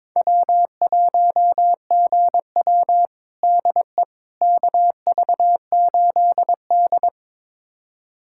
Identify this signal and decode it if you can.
Morse code, heard as a keyed tone, about 21 wpm, 700 Hz. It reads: W1GW DE KV8D
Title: USA Callsign DE Callsign - 4 Characters - 22wpm